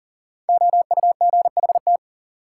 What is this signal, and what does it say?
Morse code, heard as a keyed tone, about 38 wpm, 700 Hz. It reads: OUGHT
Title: Sets of 5 Words Encore - New Word Drill - 40wpm